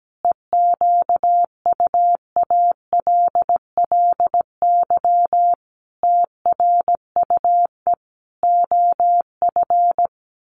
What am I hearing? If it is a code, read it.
Morse code, heard as a keyed tone, about 17 wpm, 700 Hz. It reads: EQUALLY TRUE OF
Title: Sentences from Top 1400 Words 17wpm